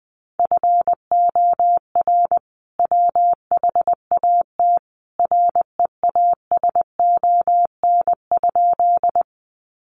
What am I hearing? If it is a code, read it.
Morse code, heard as a keyed tone, about 20 wpm, 700 Hz. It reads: FOR WHAT REASON?
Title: Sentences from Top 700 Words 20wpm